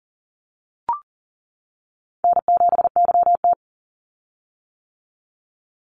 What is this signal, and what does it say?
Morse code, heard as a keyed tone, about 40 wpm, 700 Hz. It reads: N7YT
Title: Easy Call Signs - 40wpm